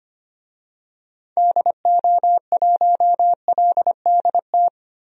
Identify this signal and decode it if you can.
Morse code, heard as a keyed tone, about 25 wpm, 700 Hz. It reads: DO1LDT
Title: DXCC-100 Call Signs - 6 Characters - Repeated 2x - 25wpm